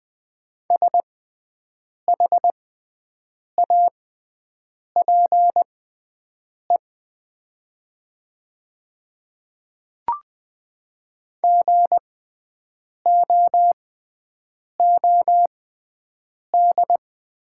Morse code, heard as a keyed tone, about 20 wpm, spaced as for 7 wpm, 700 Hz. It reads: SHAPE GOOD
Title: 6x Character Spacing - Top 500 Words - 20wpm